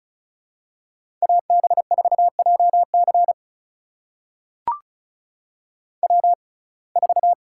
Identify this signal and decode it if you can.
Morse code, heard as a keyed tone, about 36 wpm, spaced as for 25 wpm, 700 Hz. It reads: AB4JC W4
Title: USA Call Signs - 5 Characters - 6x Character Spacing - 35wpm